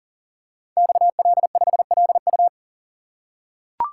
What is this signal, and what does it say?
Morse code, heard as a keyed tone, about 41 wpm, 700 Hz. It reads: XL5LU
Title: Code Groups - 5 Letter:Numbers - 40wpm